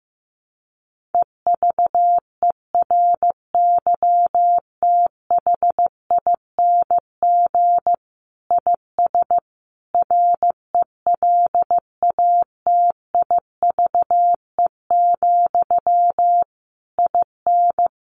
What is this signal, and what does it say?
Morse code, heard as a keyed tone, about 15 wpm, 700 Hz. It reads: EVERYTHING IS RELATIVE, IN
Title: Sentences from Top 1400 Words 15wpm